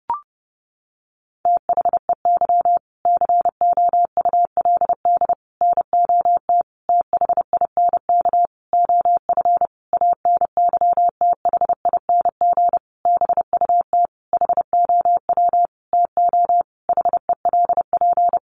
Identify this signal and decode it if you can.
Morse code, heard as a keyed tone, about 30 wpm, 700 Hz. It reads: THEY COULD NOT THINK OF ANYTHING BUT HOW TO HELP
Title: Sentences from Top 1000 Words 30wpm